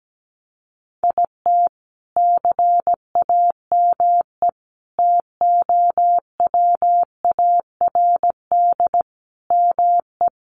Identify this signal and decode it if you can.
Morse code, heard as a keyed tone, about 17 wpm, 700 Hz. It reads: IT CAME TOWARD ME
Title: Sentences from Top 1500 Words 17wpm